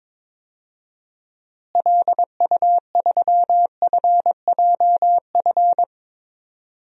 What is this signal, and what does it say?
Morse code, heard as a keyed tone, about 21 wpm, 700 Hz. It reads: LU3FJF
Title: DXCC-100 Call Signs - 6 Characters - Repeated 2x - 22wpm